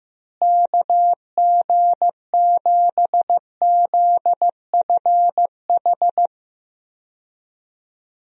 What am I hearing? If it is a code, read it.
Morse code, heard as a keyed tone, about 15 wpm, 700 Hz. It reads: KG7ZFH
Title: USA Call Signs - 6 Characters - Repeated 2x - 15wpm